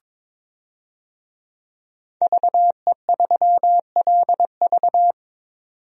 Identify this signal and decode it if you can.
Morse code, heard as a keyed tone, about 22 wpm, 700 Hz. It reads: VE3LV
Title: DXCC-100 Call Signs - 5 Characters - 22wpm